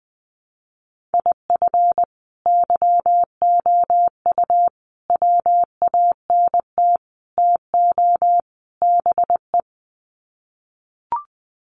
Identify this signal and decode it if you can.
Morse code, heard as a keyed tone, about 20 wpm, 700 Hz. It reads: IF YOU WANT TO BE
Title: Sets of 5 Words Encore 20wpm